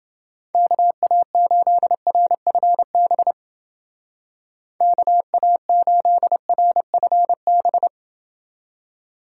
Morse code, heard as a keyed tone, about 28 wpm, 700 Hz. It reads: KA8RFB KA8RFB